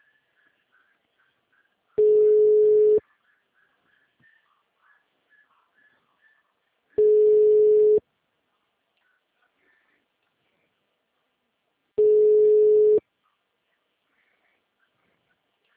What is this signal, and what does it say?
Ringback tone (Europe, 425 Hz).
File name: out- -2005- - - .33525